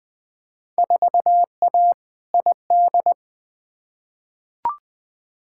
Morse code, heard as a keyed tone, about 20 wpm, 700 Hz. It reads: 4A ID
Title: Field Day Exchanges - 20wpm